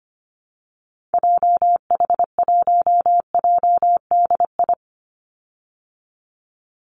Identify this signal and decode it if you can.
Morse code, heard as a keyed tone, about 25 wpm, 700 Hz. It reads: JH1JDI
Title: Call Signs Repeated 2x - 25wpm